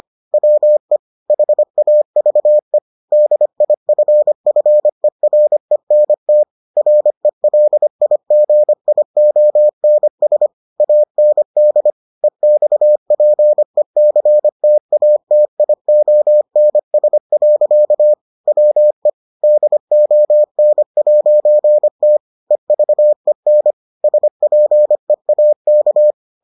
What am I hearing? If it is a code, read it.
Morse code, heard as a keyed tone, about 25 wpm, 600 Hz. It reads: WE HAVE DIFFERENT RELIGIONS AND EXPECTATIONS. WE DON'T EVEN SPEAK